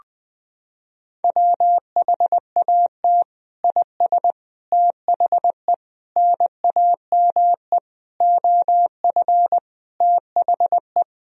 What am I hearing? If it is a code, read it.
Morse code, heard as a keyed tone, about 20 wpm, 700 Hz. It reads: WHAT IS THE NAME OF THE
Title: Sentences from Top 1000 Words - Review Part 3 - 20wpm